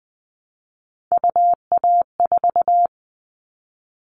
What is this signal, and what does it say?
Morse code, heard as a keyed tone, about 20 wpm, 700 Hz. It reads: UA4
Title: Callsign Prefixes for Top 50 DXCC - 20wpm